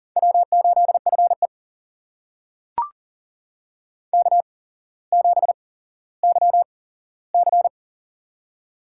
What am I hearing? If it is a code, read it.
Morse code, heard as a keyed tone, about 40 wpm, spaced as for 17 wpm, 700 Hz. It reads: W8FE K7YC
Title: USA Call Signs - 4 Characters - 8x Character Spacing - 40wpm